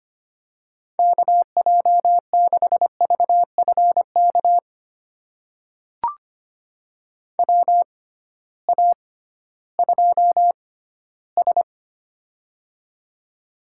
Morse code, heard as a keyed tone, about 25 wpm, spaced as for 15 wpm, 700 Hz. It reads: KJ6VFK WA2S
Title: USA Call Signs - 6x Character Spacing - 25wpm